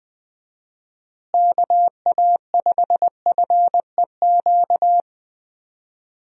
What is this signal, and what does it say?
Morse code, heard as a keyed tone, about 20 wpm, 700 Hz. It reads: KA5FEQ
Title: USA Call Signs - 6 Characters - Repeated 3x - 20wpm